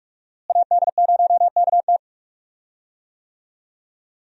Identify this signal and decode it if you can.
Morse code, heard as a keyed tone, about 42 wpm, 700 Hz. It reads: AD0KT
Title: USA Call Signs - 5 Characters - Repeated 2x - 45wpm